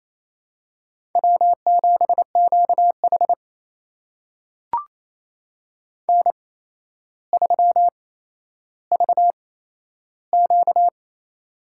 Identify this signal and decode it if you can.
Morse code, heard as a keyed tone, about 28 wpm, spaced as for 12 wpm, 700 Hz. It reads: W7QH N3VQ
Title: USA Call Signs - 4 Characters - 8x Character Spacing - 28wpm